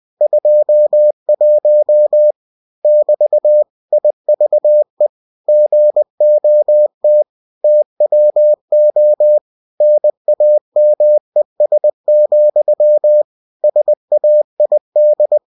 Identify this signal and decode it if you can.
Morse code, heard as a keyed tone, about 20 wpm, 600 Hz. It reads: 21 <BT> IVE GOT TWO NAMES, SAID
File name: cw-books / winnie-the-pooh / winnie-20wpm-page-0021